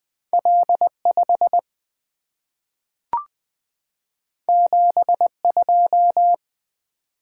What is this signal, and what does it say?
Morse code, heard as a keyed tone, about 20 wpm, 700 Hz. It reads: L 5 7 2
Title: Two Letter-Number Combinations - 20wpm